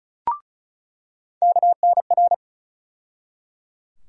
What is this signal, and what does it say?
Morse code, heard as a keyed tone, about 34 wpm, 700 Hz. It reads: KNR
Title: USA Call Sign Postfixes - 3 Letters 35wpm